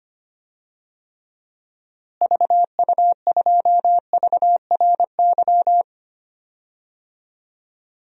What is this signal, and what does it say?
Morse code, heard as a keyed tone, about 25 wpm, 700 Hz. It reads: VU2VRY
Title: DXCC-100 Call Signs - 6 Characters - Repeated 2x - 25wpm